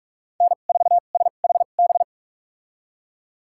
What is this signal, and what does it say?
Morse code, heard as a keyed tone, about 47 wpm, spaced as for 33 wpm, 690 Hz. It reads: N4SHB
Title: USA Call Signs - 5 Characters - 2x Character Spacing - 45wpm